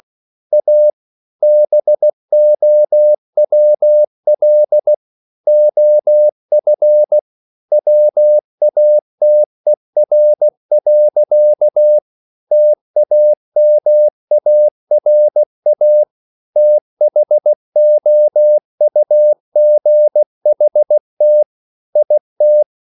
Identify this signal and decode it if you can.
Morse code, heard as a keyed tone, about 16 wpm, 600 Hz. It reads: A BOWL OF WATER. TAMARA THOUGHT IT